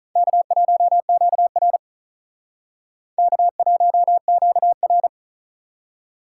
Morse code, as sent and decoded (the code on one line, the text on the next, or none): -.- .---- --.- .-. / -.- .---- --.- .-.
K1QR K1QR